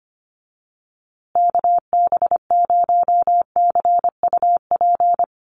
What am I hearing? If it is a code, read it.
Morse code, heard as a keyed tone, about 25 wpm, 700 Hz. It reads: KB0CUP
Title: USA Call Signs Repeated 3x 25wpm